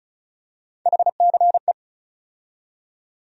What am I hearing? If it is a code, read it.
Morse code, heard as a keyed tone, about 36 wpm, 700 Hz. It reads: HCE
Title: USA Call Sign Postfixes - 3 Letters 35wpm